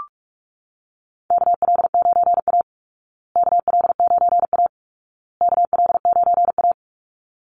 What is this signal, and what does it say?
Morse code, heard as a keyed tone, about 44 wpm, 700 Hz. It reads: KL9A KL9A KL9A
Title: Call Signs Repeated 2x - 45wpm